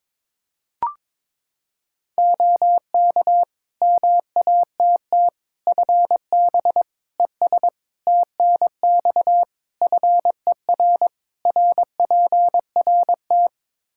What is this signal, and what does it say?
Morse code, heard as a keyed tone, about 21 wpm, 700 Hz. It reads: OK MATT FB ES TNX FER RPRT
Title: Practice QSO - Exchange 2 - Info - 22wpm